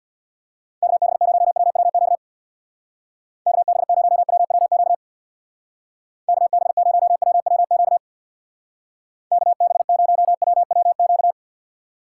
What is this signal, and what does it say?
Morse code, heard as a keyed tone, about 62 wpm, 700 Hz. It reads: KB0WWQ KB0WWQ KB0WWQ KB0WWQ
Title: Speed Racing - USA Call Signs 50wpm